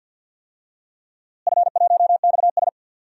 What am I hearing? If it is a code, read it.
Morse code, heard as a keyed tone, about 47 wpm, 700 Hz. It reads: F1XS
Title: DXCC-100 Call Signs - 4 Characters - Repeated 2x - 50wpm